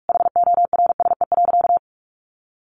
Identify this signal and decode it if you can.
Morse code, heard as a keyed tone, about 44 wpm, 700 Hz. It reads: HORSE.
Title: Sentences from Top 1500 Words 45wpm